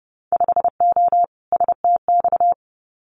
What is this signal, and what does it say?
Morse code, heard as a keyed tone, about 30 wpm, 700 Hz. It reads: 5O STX
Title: Winter Field Day Exchanges 30wpm